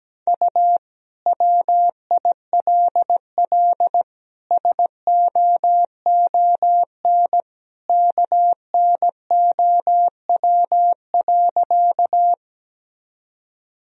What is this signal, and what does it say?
Morse code, heard as a keyed tone, about 17 wpm, 700 Hz. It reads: U WILL SOON KNOW.
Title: Sentences from Top 400 Words and Abbreviations 17wpm